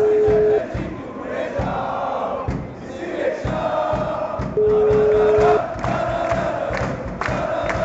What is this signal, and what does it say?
Ringback tone (Europe, 425 Hz), with noise.